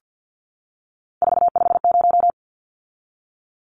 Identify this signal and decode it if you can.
Morse code, heard as a keyed tone, about 50 wpm, 700 Hz. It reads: 450
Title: Numbers - 3-Digit 50wpm